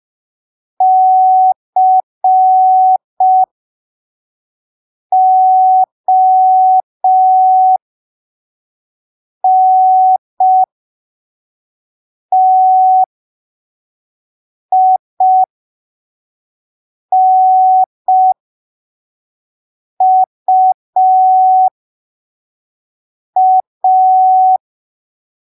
Morse code, heard as a keyed tone, about 5 wpm, spaced as for 3 wpm, 750 Hz. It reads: CONTINUA
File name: arquivos / texto / texto 22 05ppm